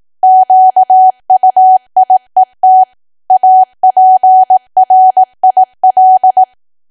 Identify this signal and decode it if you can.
Morse code, heard as a keyed tone, about 18 wpm, 750 Hz. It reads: QUIET APRIL
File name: cwa / W1AW-Level2 / 18WPM